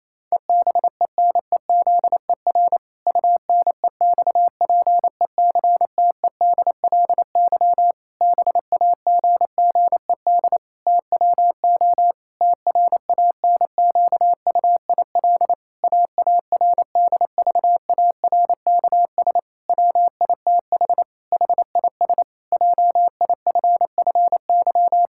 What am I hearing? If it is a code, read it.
Morse code, heard as a keyed tone, about 28 wpm, 700 Hz. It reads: EBENEZER UNEXPECTEDLY BAGGED TWO TRANQUIL AARDVARKS WITH HIS JIFFY
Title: Pangram Challenge v2 - 28wpm